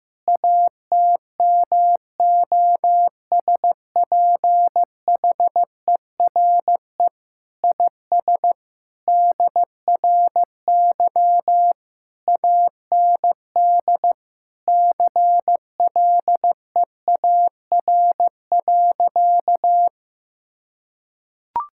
Morse code, heard as a keyed tone, about 15 wpm, 700 Hz. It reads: ATMOSPHERE IS DRY AND CLEAR.
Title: Sentences from Top 1600 Words 15wpm